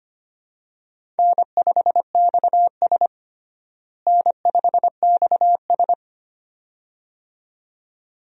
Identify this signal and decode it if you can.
Morse code, heard as a keyed tone, about 25 wpm, 700 Hz. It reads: N5XS N5XS
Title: USA Call Signs - 4 Characters - Repeated 2x - 25wpm